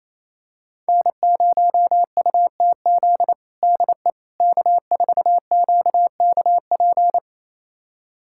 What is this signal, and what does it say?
Morse code, heard as a keyed tone, about 28 wpm, 700 Hz. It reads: N0UTZ DE K4QKP
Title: USA Callsign DE Callsign - 5 Characters - 28wpm